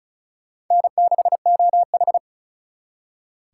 Morse code, heard as a keyed tone, about 45 wpm, 700 Hz. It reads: N6OH